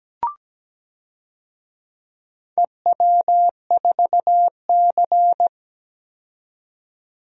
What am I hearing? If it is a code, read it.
Morse code, heard as a keyed tone, about 17 wpm, 700 Hz. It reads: EW4C
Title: DXCC-100 Call Signs - 4 Characters - 17wpm